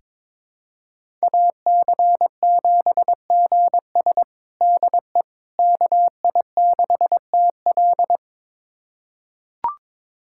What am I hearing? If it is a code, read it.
Morse code, heard as a keyed tone, about 22 wpm, 700 Hz. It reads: AC7GS DE KI6TL
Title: USA Callsign DE Callsign - 5 Characters - 22wpm